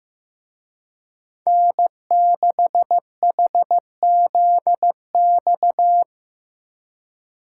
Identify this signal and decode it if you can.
Morse code, heard as a keyed tone, about 18 wpm, 700 Hz. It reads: N6HZX
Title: Speed Racing - USA 1x3 Call Signs 15wpm